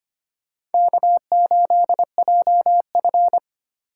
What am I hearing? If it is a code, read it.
Morse code, heard as a keyed tone, about 33 wpm, 700 Hz. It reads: K8JF